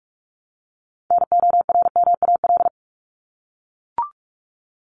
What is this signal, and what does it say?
Morse code, heard as a keyed tone, about 44 wpm, 700 Hz. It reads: NORMAL
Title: Top 1001-1250 Words 45wpm